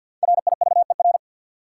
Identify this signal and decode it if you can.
Morse code, heard as a keyed tone, about 53 wpm, 700 Hz. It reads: RIVER
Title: Sentences from Top 300 Words - New Word Drill - 50wpm